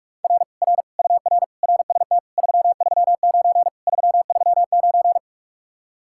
Morse code, heard as a keyed tone, about 49 wpm, 700 Hz. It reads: R R UR RST 339 339